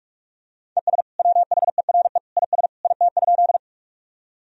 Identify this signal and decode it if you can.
Morse code, heard as a keyed tone, about 46 wpm, 700 Hz. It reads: ES WHERE IS IT?
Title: Sentences from Top 200 Words and Abbreviations 45wpm